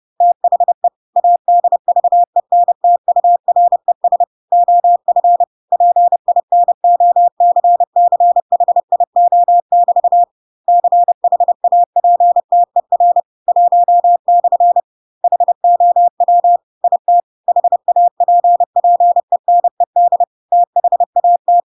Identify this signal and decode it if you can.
Morse code, heard as a keyed tone, about 30 wpm, 700 Hz. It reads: THE ADVENTURES OF PINOCCHIO<BT> CHAPTER 1/ HOW IT HAPPENED THAT
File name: Pinocchio / Pinocchio ch1 30 30 ik1hge